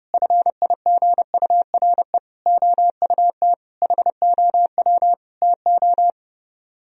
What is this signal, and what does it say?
Morse code, heard as a keyed tone, about 32 wpm, 700 Hz. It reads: FIGURE OUT HOW TO